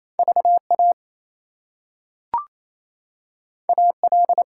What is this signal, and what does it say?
Morse code, heard as a keyed tone, about 28 wpm, 700 Hz. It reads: VA AL
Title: US States and Canadian Provinces 28wpm